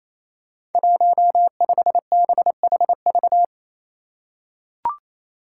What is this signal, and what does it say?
Morse code, heard as a keyed tone, about 27 wpm, 700 Hz. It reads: 15BHV
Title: Five Challenging Letter-Number Combinations 28wpm